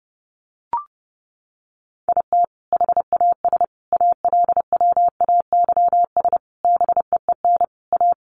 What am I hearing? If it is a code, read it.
Morse code, heard as a keyed tone, about 30 wpm, 700 Hz. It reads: IT HAS ALWAYS BEEN A
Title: Sentences from Top 1300 Words 30wpm